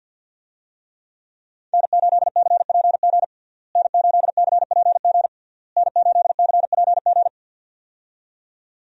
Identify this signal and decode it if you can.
Morse code, heard as a keyed tone, about 52 wpm, 700 Hz. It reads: N8CPG N8CPG N8CPG
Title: USA Call Signs - 5 Characters - Repeated 3x - 50wpm